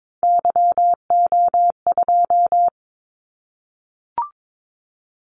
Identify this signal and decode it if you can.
Morse code, heard as a keyed tone, about 22 wpm, 700 Hz. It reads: YO2
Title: Callsign Prefixes for Top 25 DXCC - 22wpm